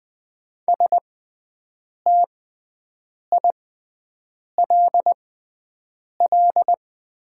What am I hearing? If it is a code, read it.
Morse code, heard as a keyed tone, about 20 wpm, spaced as for 7 wpm, 700 Hz. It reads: STILL
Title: 6x Character Spacing - Top 300 Words - 20wpm